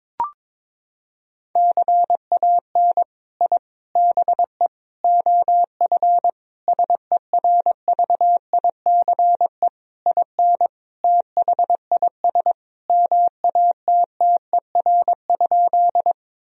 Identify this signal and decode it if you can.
Morse code, heard as a keyed tone, about 22 wpm, 700 Hz. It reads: CAN I BE OF SERVICE IN THIS MATTER?
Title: Sentences from Top 1000 Words - Review Part 4 - 22wpm